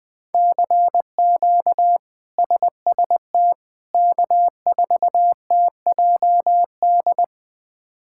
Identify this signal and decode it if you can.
Morse code, heard as a keyed tone, about 21 wpm, 700 Hz. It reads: CQ SST K4TJD
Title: K1USN SST QSO Practice - 20wpm